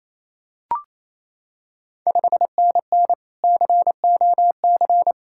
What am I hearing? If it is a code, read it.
Morse code, heard as a keyed tone, about 27 wpm, 690 Hz. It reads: 5NN COC
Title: Ontario QSO Party Exchange 28wpm